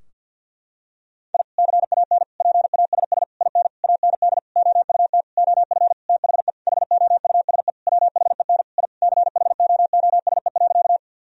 Morse code, heard as a keyed tone, about 52 wpm, 700 Hz. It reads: I CAN PASS IN AND OUT OF THE HOUSE WHEN I CHOOSE.
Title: Sentences from Top 1000 Words - Review Part 4 - 50wpm